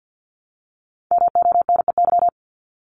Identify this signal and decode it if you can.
Morse code, heard as a keyed tone, about 50 wpm, 700 Hz. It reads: MONEY
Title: Multi-letter Phonemes Repeated 3x 50wpm